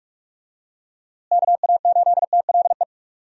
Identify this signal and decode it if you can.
Morse code, heard as a keyed tone, about 41 wpm, 700 Hz. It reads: KA8TLE
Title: USA Call Signs - 6 Characters - Repeated 2x - 45wpm